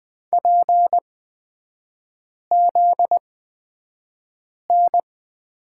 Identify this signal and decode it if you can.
Morse code, heard as a keyed tone, about 20 wpm, spaced as for 5 wpm, 700 Hz. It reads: PZN
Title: 243 Lesson 39e - Letter z - Three Characters 20wpm